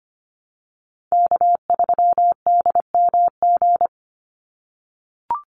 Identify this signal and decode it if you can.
Morse code, heard as a keyed tone, about 25 wpm, 700 Hz. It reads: K3DMG
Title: Call Signs - 25wpm